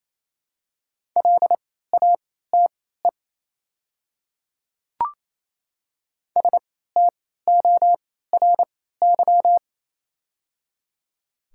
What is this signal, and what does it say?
Morse code, heard as a keyed tone, about 28 wpm, spaced as for 16 wpm, 700 Hz. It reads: LATE STORY
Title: 3x Character Spacing - Top 300 Words - 28wpm